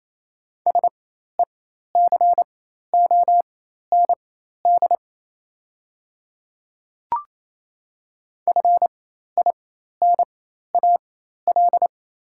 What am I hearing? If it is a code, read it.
Morse code, heard as a keyed tone, about 28 wpm, spaced as for 13 wpm, 700 Hz. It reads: SECOND FINAL
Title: 4x Character Spacing - Top 500 Words - 28wpm